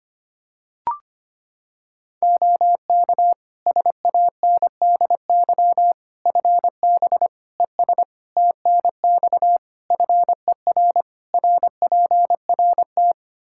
Morse code, heard as a keyed tone, about 25 wpm, 700 Hz. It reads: OK SANDY FB ES TNX FER RPRT
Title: K1USN SST QSO Practice 25wpm